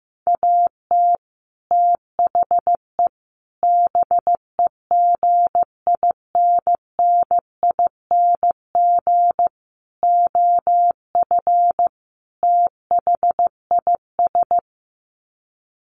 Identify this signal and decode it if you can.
Morse code, heard as a keyed tone, about 15 wpm, 700 Hz. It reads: AT THE BEGINNING OF THIS
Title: Sets of 5 Words Encore 15wpm